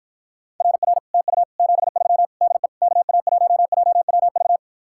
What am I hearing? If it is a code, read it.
Morse code, heard as a keyed tone, about 54 wpm, 700 Hz. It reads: RR TU 73 DE KA1JWV